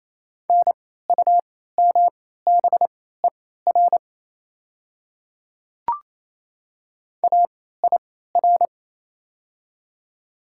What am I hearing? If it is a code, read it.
Morse code, heard as a keyed tone, about 27 wpm, spaced as for 16 wpm, 700 Hz. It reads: NUMBER AIR
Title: 3x Character Spacing - Top 200 Words - 28wpm